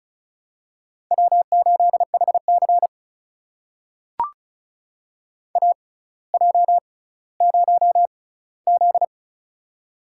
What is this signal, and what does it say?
Morse code, heard as a keyed tone, about 36 wpm, spaced as for 18 wpm, 700 Hz. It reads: W8HC AJ0Z
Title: USA Call Signs - 4 Characters - 6x Character Spacing - 35wpm